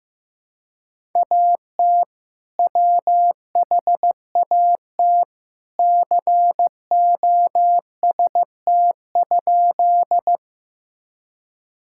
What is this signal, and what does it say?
Morse code, heard as a keyed tone, about 15 wpm, 700 Hz. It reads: AT WHAT COST?
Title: Sentences from Top 1000 Words - Review Part 5 - 15wpm